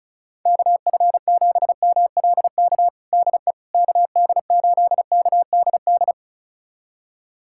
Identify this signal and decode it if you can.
Morse code, heard as a keyed tone, about 36 wpm, 700 Hz. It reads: KF7MLK DE KD8KDD
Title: USA Callsign DE Callsign 35wpm